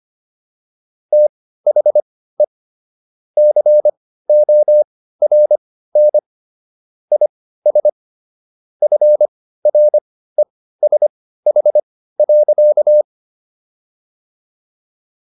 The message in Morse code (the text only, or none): THE CORN IS FRESH.